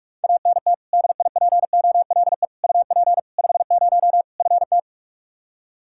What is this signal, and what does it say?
Morse code, heard as a keyed tone, about 47 wpm, 700 Hz. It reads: ANT DIPOLE UP 50 FT
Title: Practice QSO - Exchange 2 - Info - 45wpm